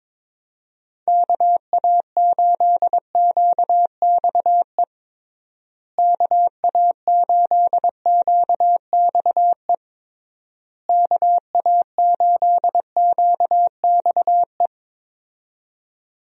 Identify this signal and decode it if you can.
Morse code, heard as a keyed tone, about 22 wpm, 700 Hz. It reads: KA8QXE KA8QXE KA8QXE
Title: USA Call Signs - 6 Characters - Repeated 3x - 22wpm